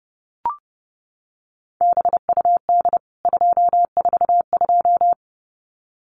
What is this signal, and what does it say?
Morse code, heard as a keyed tone, about 30 wpm, 700 Hz. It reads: BUD 242
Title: CWT Exchange 30wpm